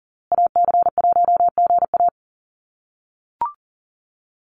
Morse code, heard as a keyed tone, about 40 wpm, 700 Hz. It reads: AC1GA
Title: USA Call Signs Repeated 2x 40wpm